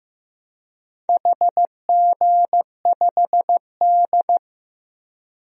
Morse code, heard as a keyed tone, about 15 wpm, 700 Hz. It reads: HG5D
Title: DXCC-100 Call Signs - 4 Characters - Repeated 3x - 15wpm